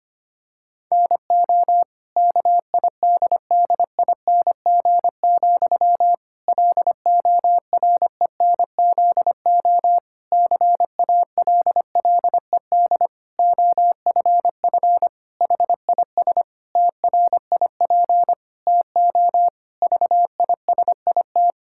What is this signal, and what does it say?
Morse code, heard as a keyed tone, about 25 wpm, 700 Hz. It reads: NO KIDDING, LORENZO CALLED OFF HIS TRIP TO VISIT
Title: Pangram Challenge v2 - 25wpm